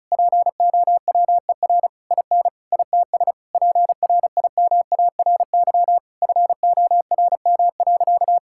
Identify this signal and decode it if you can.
Morse code, heard as a keyed tone, about 34 wpm, 700 Hz. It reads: POWER IN ITS PRIMARY FORM.
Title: Sentences from Top 1000 Words - Review Part 5 - 35wpm